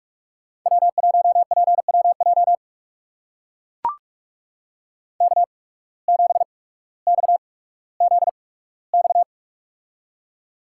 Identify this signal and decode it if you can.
Morse code, heard as a keyed tone, about 43 wpm, spaced as for 19 wpm, 700 Hz. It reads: W1PWJ K7XZX
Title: USA Call Signs - 5 Characters - 8x Character Spacing - 45wpm